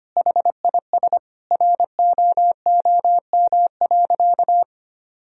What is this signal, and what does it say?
Morse code, heard as a keyed tone, about 25 wpm, 700 Hz. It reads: HIS ROOM.
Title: Sentences from Top 1000 Words - Review Part 4 - 25wpm